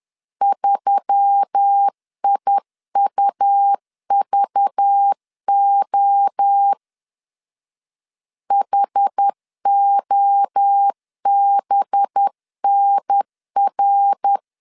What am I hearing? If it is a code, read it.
Morse code, heard as a keyed tone, about 10 wpm, 800 Hz. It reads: 3IUVO HOBNR
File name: LEZIONI / 6 PKX0 / 6 PKX0 10wpm 2707